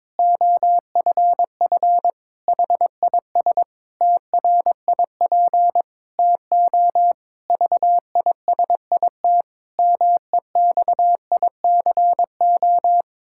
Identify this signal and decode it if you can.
Morse code, heard as a keyed tone, about 22 wpm, 700 Hz. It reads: OFF HIS TRIP TO VISIT MEXICO
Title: Pangram Challenge v2 - 22wpm